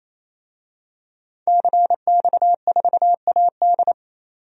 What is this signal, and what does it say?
Morse code, heard as a keyed tone, about 28 wpm, 700 Hz. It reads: CX4AD
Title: DXCC-100 Call Signs - 5 Characters - 28wpm